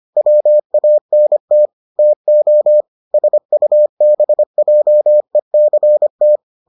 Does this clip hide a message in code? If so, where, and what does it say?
Yes: Morse code from 0.2 to 6.4 s, heard as a keyed tone, about 25 wpm, 600 Hz. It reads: WANT TO SUBJECT